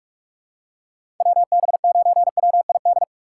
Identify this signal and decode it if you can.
Morse code, heard as a keyed tone, about 46 wpm, 690 Hz. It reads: WB9WID
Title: USA Call Signs - 6 Characters - 2x Character Spacing - 45wpm